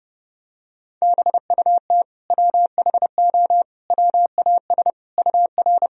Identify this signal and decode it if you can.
Morse code, heard as a keyed tone, about 30 wpm, 700 Hz. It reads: BUT WHO WAS UR
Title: Sentences from Top 200 Words and Abbreviations 30wpm